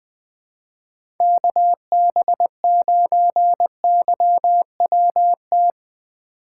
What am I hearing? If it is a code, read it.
Morse code, heard as a keyed tone, about 21 wpm, 700 Hz. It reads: KB9YWT
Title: USA Call Signs - 6 Characters - 4x Character Spacing - 20wpm